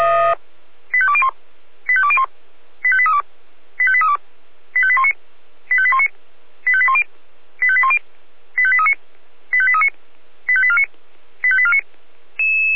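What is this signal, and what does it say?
Five-tone selective call (ZVEI), heard as quick runs of five tones, tones of about 70 ms. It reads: 85201 85201 85832 85832 85819 85819 85810 85810 85839 85839 85849 85849
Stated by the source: ZVEI five-tone selective call received on SDR.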